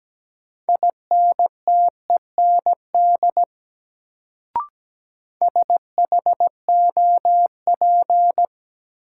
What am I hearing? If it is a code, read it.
Morse code, heard as a keyed tone, about 17 wpm, 700 Hz. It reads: INTEND SHOP
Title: Sentences from Top 1600 Words - New Word Drill Alt - 17wpm